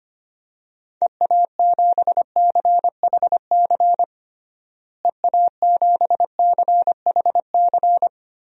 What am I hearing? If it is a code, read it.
Morse code, heard as a keyed tone, about 25 wpm, 700 Hz. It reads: EA7CHC EA7CHC
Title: DXCC-100 Call Signs - 6 Characters - Repeated 3x - 25wpm